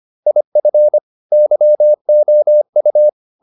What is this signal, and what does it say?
Morse code, heard as a keyed tone, about 25 wpm, 600 Hz. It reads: IF YOU